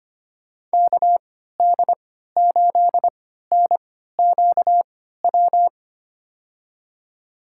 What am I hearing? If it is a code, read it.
Morse code, heard as a keyed tone, about 25 wpm, spaced as for 14 wpm, 700 Hz. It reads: KD8NQW
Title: USA Call Signs - 6 Characters - 3x Character Spacing - 25wpm